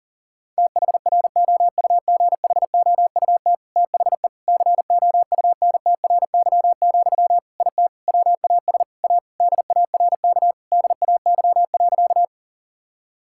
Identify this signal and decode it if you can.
Morse code, heard as a keyed tone, about 37 wpm, 700 Hz. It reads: THROUGHOUT THE COUNTRY, IT WAS A DARK DAY.
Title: Sentences from Top 1000 Words - Review Part 5 - 40wpm